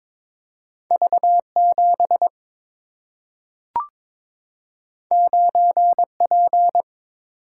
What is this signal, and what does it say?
Morse code, heard as a keyed tone, about 22 wpm, 700 Hz. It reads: V7 9P
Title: Single-Letter Single-Number 22wpm